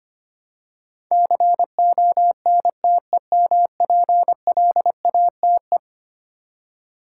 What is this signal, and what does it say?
Morse code, heard as a keyed tone, about 25 wpm, 700 Hz. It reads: CONTEMPLATE
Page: morse-code-ninja-practice-25wpm